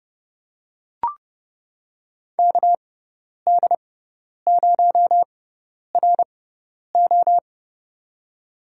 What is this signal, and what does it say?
Morse code, heard as a keyed tone, about 30 wpm, spaced as for 10 wpm, 700 Hz. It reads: KD0RO
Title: USA Call Signs - 5 Characters - 6x Character Spacing - 30wpm